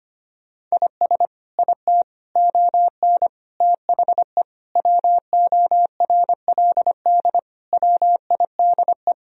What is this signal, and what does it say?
Morse code, heard as a keyed tone, about 25 wpm, 700 Hz. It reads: IS IT ON THE WORLD WIDE
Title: Sentences from Top 1000 Words - Review Part 4 - 25wpm